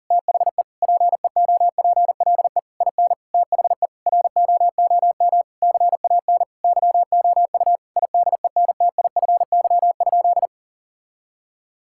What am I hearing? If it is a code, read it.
Morse code, heard as a keyed tone, about 40 wpm, 690 Hz. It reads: THE PEOPLE IN THE ROOM CAN YOU IDENTIFY?
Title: Sentences from Top 1000 Words - Review Part 6 - 40wpm